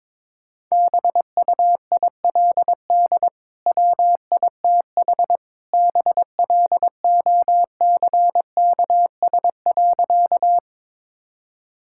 Morse code, heard as a keyed tone, about 22 wpm, 700 Hz. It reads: BUILD WITH BLOCKS.
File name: practice / 22wpm / Easy Sentences with Hints - 22wpm